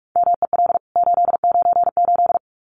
Morse code, heard as a keyed tone, about 44 wpm, 700 Hz. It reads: MEL 898
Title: CWT Exchange 45wpm